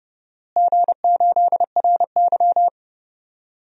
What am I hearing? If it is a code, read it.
Morse code, heard as a keyed tone, about 30 wpm, 700 Hz. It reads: G8RY
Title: DXCC-100 Call Signs - 4 Characters - Repeated 3x - 30wpm